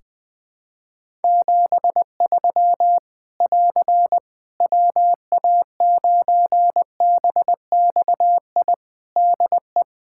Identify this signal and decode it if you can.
Morse code, heard as a keyed tone, about 21 wpm, 700 Hz. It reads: 73 <AR> WA9BXI DE
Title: Practice QSO - Exchange 4 - Ending - 20wpm